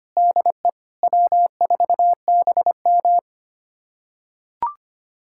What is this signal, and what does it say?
Morse code, heard as a keyed tone, about 25 wpm, 700 Hz. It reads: DE W4BM
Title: USA Callsign DE Callsign - 4 Characters - 25wpm